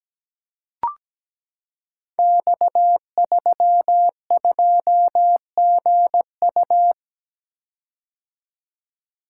Morse code, heard as a keyed tone, about 17 wpm, 700 Hz. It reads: X32GU
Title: Code Groups - 5 Letter:Numbers - 17wpm